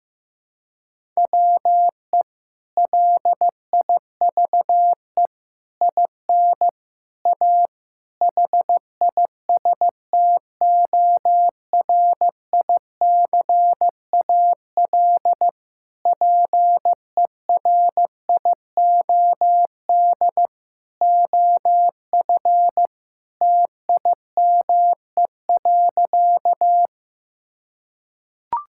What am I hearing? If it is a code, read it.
Morse code, heard as a keyed tone, about 16 wpm, 700 Hz. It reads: WE LIVE IN A HISTORICAL PERIOD OF TIME.
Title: Sentences from Top 1000 Words - Review Part 5 - 15wpm